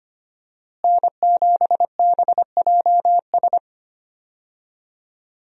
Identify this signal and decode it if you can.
Morse code, heard as a keyed tone, about 25 wpm, 700 Hz. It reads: N7BJS
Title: USA Call Signs - 5 Characters - Repeated 2x - 25wpm